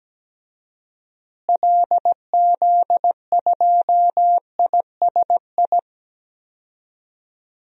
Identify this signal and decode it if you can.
Morse code, heard as a keyed tone, about 17 wpm, 700 Hz. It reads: LZ2ISI
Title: DXCC-100 Call Signs - 6 Characters - Repeated 3x - 17wpm